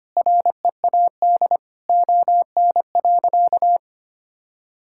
Morse code, heard as a keyed tone, about 25 wpm, 700 Hz. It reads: READ ON.
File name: practice / 25wpm / Sentences with 9-Word Phrases - 25wpm